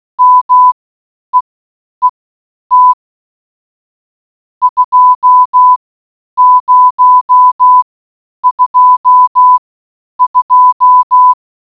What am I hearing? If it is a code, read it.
Morse code, heard as a keyed tone, about 16 wpm, spaced as for 9 wpm, 1000 Hz. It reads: MEET 2022